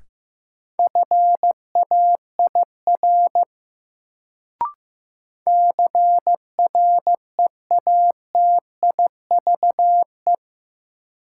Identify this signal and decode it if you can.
Morse code, heard as a keyed tone, about 15 wpm, 700 Hz. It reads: FAIR CREATIVE
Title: Sentences from Top 1000 Words - Review Part 5 - New Word Drill - 15wpm